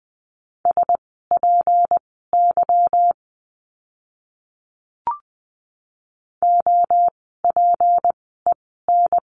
Morse code, heard as a keyed tone, about 20 wpm, spaced as for 14 wpm, 700 Hz. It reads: SPY OPEN
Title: 153 Lesson 25f - Letter p - Words 20wpm